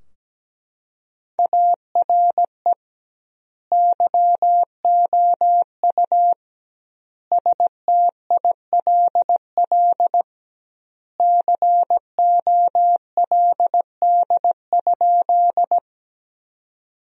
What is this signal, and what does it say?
Morse code, heard as a keyed tone, about 17 wpm, 700 Hz. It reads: ARE YOU STILL COLD?
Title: 2x Word-Spacing - Sentences from Top 400 Words - 17wpm